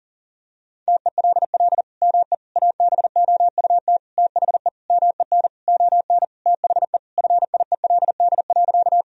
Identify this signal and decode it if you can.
Morse code, heard as a keyed tone, about 37 wpm, 700 Hz. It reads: TELL ME ABOUT THE MEN ON THE FIELD.
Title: Sentences from Top 1000 Words - Review Part 2 - 40wpm